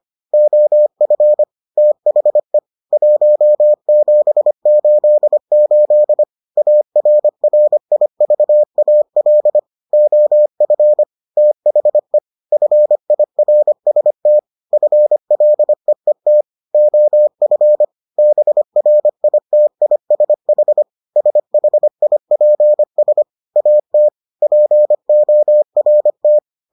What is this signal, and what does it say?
Morse code, heard as a keyed tone, about 25 wpm, 600 Hz. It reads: OF THE 1788 ARRIVAL OF THE FIRST FLEET OF BRITISH SHIPS AT PORT